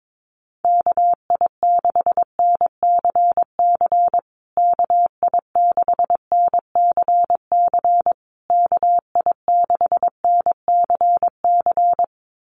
Morse code, heard as a keyed tone, about 22 wpm, 700 Hz. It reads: KI6NCC KI6NCC KI6NCC
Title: USA Call Signs Repeated 3x 22wpm